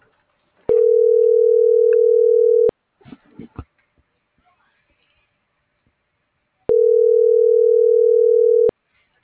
Ringback tone (North America), 440 + 480 Hz.